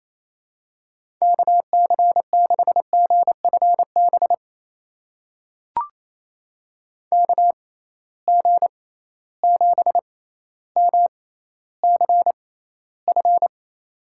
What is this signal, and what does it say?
Morse code, heard as a keyed tone, about 28 wpm, spaced as for 14 wpm, 700 Hz. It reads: KC6GFB KG7MCF
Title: USA Call Signs - 6 Characters - 6x Character Spacing - 28wpm